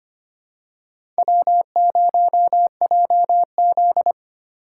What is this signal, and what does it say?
Morse code, heard as a keyed tone, about 25 wpm, 700 Hz. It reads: W0JZ
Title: USA Call Signs - 4 Characters - 3x Character Spacing - 25wpm